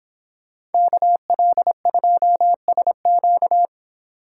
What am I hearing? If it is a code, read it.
Morse code, heard as a keyed tone, about 26 wpm, 700 Hz. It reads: KL2SQ